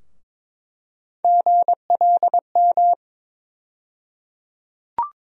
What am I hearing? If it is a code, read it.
Morse code, heard as a keyed tone, about 21 wpm, 700 Hz. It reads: GLM
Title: USA Call Sign Postfixes - 3 Letters 22wpm